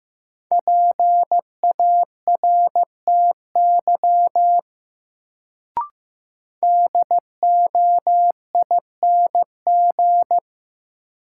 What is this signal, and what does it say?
Morse code, heard as a keyed tone, about 16 wpm, 700 Hz. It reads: PARTY DOING
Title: Sentences from Top 1400 Words - New Word Drill Alt - 15wpm